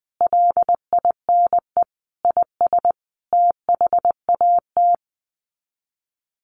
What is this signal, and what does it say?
Morse code, heard as a keyed tone, about 20 wpm, 700 Hz. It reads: LINE IS THAT
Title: Sets of 5 Words Encore 20wpm